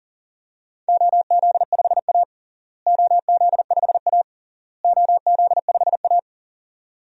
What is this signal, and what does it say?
Morse code, heard as a keyed tone, about 37 wpm, 700 Hz. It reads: OZ5A OZ5A OZ5A
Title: DXCC-100 Call Signs - 4 Characters - Repeated 3x - 40wpm